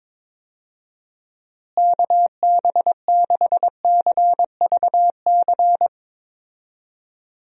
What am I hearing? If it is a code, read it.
Morse code, heard as a keyed tone, about 22 wpm, 700 Hz. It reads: KB6CVC
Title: USA Call Signs - 6 Characters - Repeated 2x - 22wpm